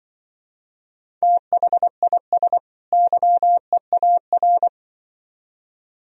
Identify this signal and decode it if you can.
Morse code, heard as a keyed tone, about 24 wpm, 700 Hz. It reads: THIS YEAR
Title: Speed-Racing - Sets of 2 Words 20wpm